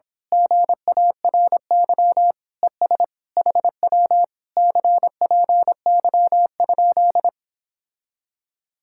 Morse code, heard as a keyed tone, about 26 wpm, 700 Hz. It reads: GARY ES HW CPY?
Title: Speed-Racing Common QSO Phrases - 22wpm